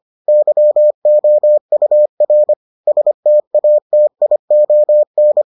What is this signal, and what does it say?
Morse code, heard as a keyed tone, about 25 wpm, 600 Hz. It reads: YOUR STATION